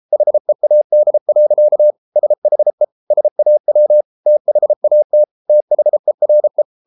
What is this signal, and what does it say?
Morse code, heard as a keyed tone, about 33 wpm, 600 Hz. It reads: HEAD. SHE SAW THAT THERE